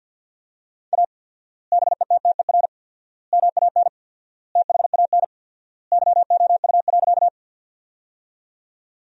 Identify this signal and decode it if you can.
Morse code, heard as a keyed tone, about 52 wpm, 700 Hz. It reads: A BETTER MAN THAN YOU.
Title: 4x Word-Spacing - Sentences from Top 400 Words - 50wpm